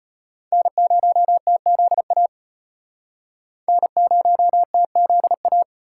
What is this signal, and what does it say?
Morse code, heard as a keyed tone, about 32 wpm, 700 Hz. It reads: N0TZA N0TZA